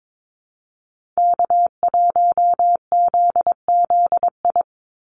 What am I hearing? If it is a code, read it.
Morse code, heard as a keyed tone, about 22 wpm, 700 Hz. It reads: K1ZZI
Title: Call Signs - 22wpm